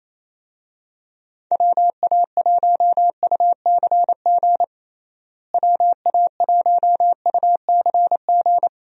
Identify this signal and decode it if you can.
Morse code, heard as a keyed tone, about 28 wpm, 700 Hz. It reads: WA1UCG WA1UCG
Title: USA Call Signs - 6 Characters - Repeated 2x - 28wpm